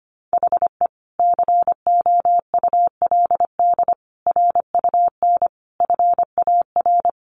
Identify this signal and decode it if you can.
Morse code, heard as a keyed tone, about 25 wpm, 700 Hz. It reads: HE COULD RUN FAR
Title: Sentences from Top 1100 Words 25wpm